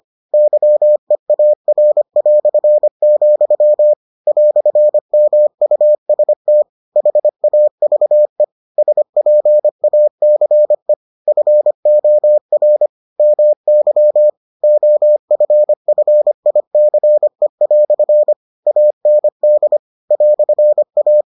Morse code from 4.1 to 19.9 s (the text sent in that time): "MUST HAVE SPACE FOR MY OFFICE" AND